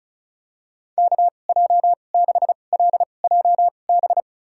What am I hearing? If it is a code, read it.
Morse code, heard as a keyed tone, about 36 wpm, spaced as for 25 wpm, 700 Hz. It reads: KJ6LJB
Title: USA Call Signs - 2x Character Spacing - 35wpm